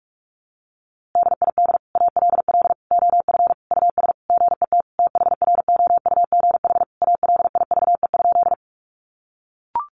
Morse code, heard as a keyed tone, about 44 wpm, 700 Hz. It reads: DID ALL OF US GET THROUGH ALIVE?
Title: A Sound Of Thunder 45wpm